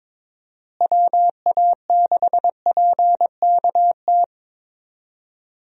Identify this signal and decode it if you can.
Morse code, heard as a keyed tone, about 22 wpm, 700 Hz. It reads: WA6PKT
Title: USA Call Signs - 6 Characters - Repeated 2x - 22wpm